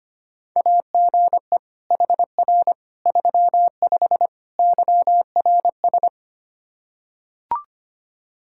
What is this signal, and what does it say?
Morse code, heard as a keyed tone, about 25 wpm, 700 Hz. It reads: AGE HR 35 YRS
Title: Practice QSO - Exchange 3 - More Info - 25wpm